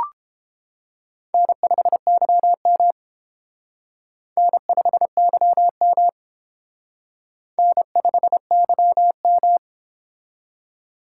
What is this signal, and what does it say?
Morse code, heard as a keyed tone, about 29 wpm, 700 Hz. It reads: N5YM N5YM N5YM